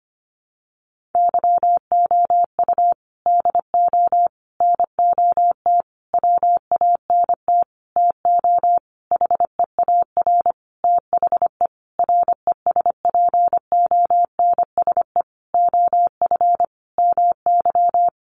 Morse code, heard as a keyed tone, about 25 wpm, 700 Hz. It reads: YOU DO NOT WANT TO HEAR THE RESPONSE OF MY
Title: Sentences from Top 1400 Words 25wpm